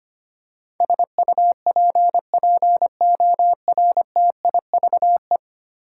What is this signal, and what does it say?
Morse code, heard as a keyed tone, about 25 wpm, 700 Hz. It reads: SUPPORTIVE
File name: Ten Letter Words - 25wpm